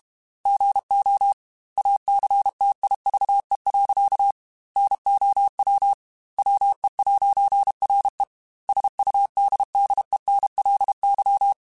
Morse code, heard as a keyed tone, about 32 wpm, 800 Hz. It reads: GO ACTIVE. NOW WE'RE SUDDENLY
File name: Technical Subjects / Technical Subjects-mp3s / AND Gate @30